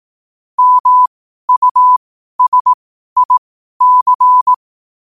Morse code, heard as a keyed tone, about 18 wpm, spaced as for 12 wpm, 1000 Hz. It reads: MUSIC
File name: qotdfiles / 12wpm